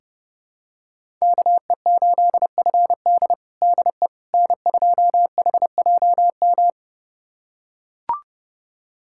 Morse code, heard as a keyed tone, about 30 wpm, 700 Hz. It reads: KE8FD DE N2HJM
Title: USA Callsign DE Callsign - 5 Characters - 30wpm